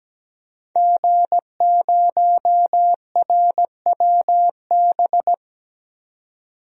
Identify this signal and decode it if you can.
Morse code, heard as a keyed tone, about 17 wpm, 700 Hz. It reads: G0RWB
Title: DXCC-100 Call Signs - 5 Characters - 17wpm